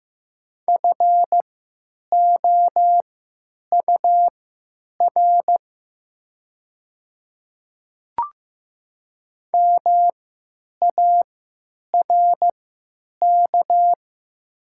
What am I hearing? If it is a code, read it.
Morse code, heard as a keyed tone, about 15 wpm, spaced as for 9 wpm, 700 Hz. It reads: FOUR MARK
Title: 3x Character Spacing - Top 300 Words - 15wpm